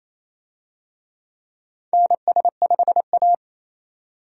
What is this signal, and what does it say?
Morse code, heard as a keyed tone, about 28 wpm, 700 Hz. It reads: NS5A
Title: USA Call Signs - 4 Characters - 28wpm